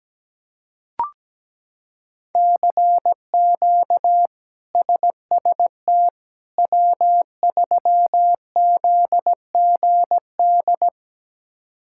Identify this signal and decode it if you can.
Morse code, heard as a keyed tone, about 17 wpm, 700 Hz. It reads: CQ SST W3ZGD
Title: K1USN SST QSO Practice - 17wpm